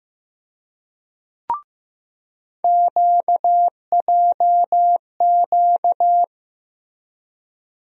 Morse code, heard as a keyed tone, about 15 wpm, 700 Hz. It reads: QJQ
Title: USA Call Sign Postfixes - 3 Letters 15wpm